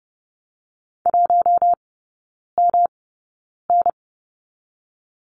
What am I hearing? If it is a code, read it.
Morse code, heard as a keyed tone, about 30 wpm, spaced as for 9 wpm, 700 Hz. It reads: 1MN
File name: Lesson 18e - Letter m - Three Characters 30wpm